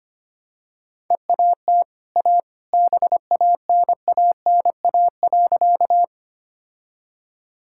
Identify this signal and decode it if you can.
Morse code, heard as a keyed tone, about 25 wpm, 700 Hz. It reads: EAT A BANANA.
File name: Easy Sentences with Hints - 25wpm